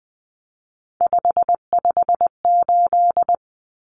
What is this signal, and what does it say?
Morse code, heard as a keyed tone, about 20 wpm, 700 Hz. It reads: 558
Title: Common Cut Numbers - 3-Digit 20wpm